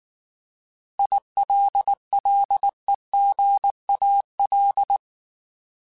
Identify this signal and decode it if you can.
Morse code, heard as a keyed tone, about 19 wpm, 800 Hz. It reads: ILLEGAL
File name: Alphabetical / Alphabetical mp3s / Is @18